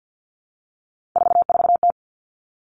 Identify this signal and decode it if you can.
Morse code, heard as a keyed tone, about 50 wpm, 700 Hz. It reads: 44T
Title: Three Challenging Letter-Number Combinations 50wpm